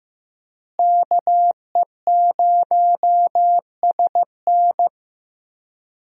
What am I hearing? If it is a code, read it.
Morse code, heard as a keyed tone, about 15 wpm, 700 Hz. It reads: KE0SN
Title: USA Call Signs - 5 Characters - Repeated 3x - 15wpm